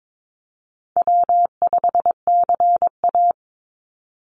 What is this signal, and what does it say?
Morse code, heard as a keyed tone, about 22 wpm, 700 Hz. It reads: W5CA
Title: USA Call Signs 22wpm